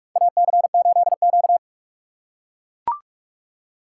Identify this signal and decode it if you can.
Morse code, heard as a keyed tone, about 44 wpm, 700 Hz. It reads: AC8Q
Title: USA Call Signs - 4 Characters - 6x Character Spacing - 45wpm